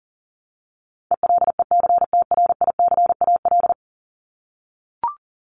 Morse code, heard as a keyed tone, about 40 wpm, 700 Hz. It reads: ELECTRICAL
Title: Top 751-1000 Words 40wpm